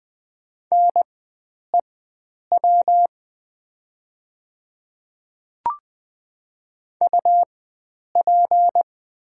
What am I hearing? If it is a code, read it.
Morse code, heard as a keyed tone, about 20 wpm, spaced as for 9 wpm, 700 Hz. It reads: NEW UP
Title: 4x Character Spacing - Top 500 Words - 20wpm